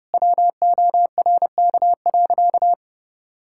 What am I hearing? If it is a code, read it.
Morse code, heard as a keyed tone, about 30 wpm, 700 Hz. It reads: WORK.
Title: Sentences from Top 1600 Words 30wpm